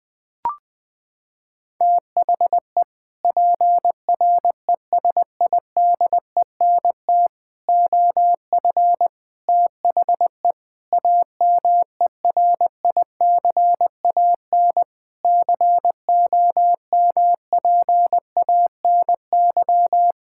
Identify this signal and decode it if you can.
Morse code, heard as a keyed tone, about 20 wpm, 700 Hz. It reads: THE PRESIDENT OF THE AMERICAN COMPANY
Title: Sentences with 3-Word Phrases - 20wpm